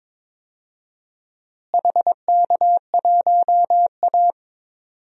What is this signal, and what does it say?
Morse code, heard as a keyed tone, about 22 wpm, 700 Hz. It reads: HK1A
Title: DXCC-100 Call Signs - 4 Characters - Repeated 3x - 22wpm